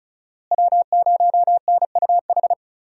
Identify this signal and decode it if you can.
Morse code, heard as a keyed tone, about 36 wpm, 700 Hz. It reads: W0NUH
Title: USA Call Signs - 5 Characters - Repeated 2x - 35wpm